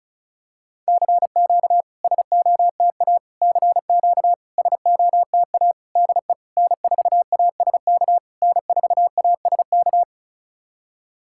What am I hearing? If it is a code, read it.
Morse code, heard as a keyed tone, about 36 wpm, 690 Hz. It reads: CQ SOTA CQ SOTA DE N4ASK N4ASK
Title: SOTA QSO Practice 35wpm